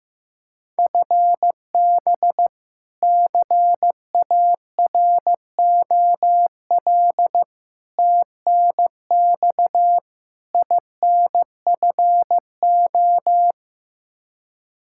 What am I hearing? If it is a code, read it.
Morse code, heard as a keyed tone, about 15 wpm, 700 Hz. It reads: FB CAROL TNX INFO
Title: Practice QSO - Exchange 3 - More Info - 15wpm